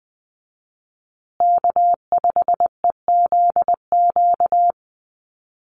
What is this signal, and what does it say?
Morse code, heard as a keyed tone, about 20 wpm, 700 Hz. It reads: K5EZQ
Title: USA Call Signs Repeated 2x 20wpm